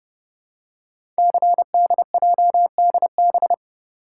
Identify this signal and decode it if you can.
Morse code, heard as a keyed tone, about 30 wpm, 700 Hz. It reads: CDJDB
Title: Code Groups - 5 Letters - 30wpm